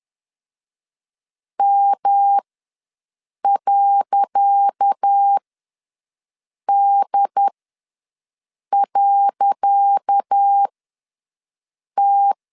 Morse code, heard as a keyed tone, about 10 wpm, spaced as for 5 wpm, 800 Hz. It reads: M.D.T
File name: LEZIONI / 10 SIGNS / 10C DOT ws 10wpm 4213